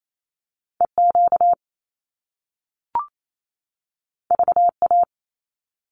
Morse code, heard as a keyed tone, about 28 wpm, 700 Hz. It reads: EQ VA
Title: Top Two Letter Combinations in Words 28wpm